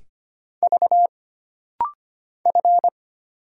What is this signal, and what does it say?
Morse code, heard as a keyed tone, about 25 wpm, 700 Hz. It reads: V F
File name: Single Character (Letter:Number:Punctuation:Prosign) - Rapid-Fire - 25wpm